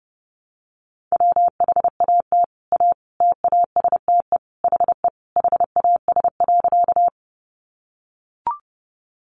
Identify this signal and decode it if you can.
Morse code, heard as a keyed tone, about 30 wpm, 700 Hz. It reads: WHAT A TASTE HE HAS.
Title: Sentences from Top 1100 Words 30wpm